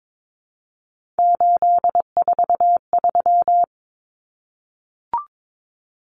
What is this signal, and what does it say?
Morse code, heard as a keyed tone, about 22 wpm, 700 Hz. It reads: 843
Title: Common Cut Numbers - 3-Digit 22wpm